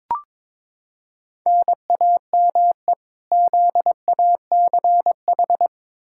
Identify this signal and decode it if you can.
Morse code, heard as a keyed tone, about 22 wpm, 700 Hz. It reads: NAME ZACH
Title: K1USN SST QSO Practice 22wpm